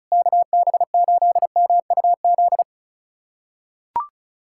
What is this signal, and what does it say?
Morse code, heard as a keyed tone, about 36 wpm, 700 Hz. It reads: KB8MUZ
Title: USA Callsign DE Callsign 35wpm